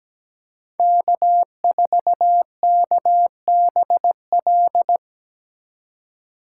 Morse code, heard as a keyed tone, about 17 wpm, 700 Hz. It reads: K4KBL
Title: K1USN SST QSO Practice - 17wpm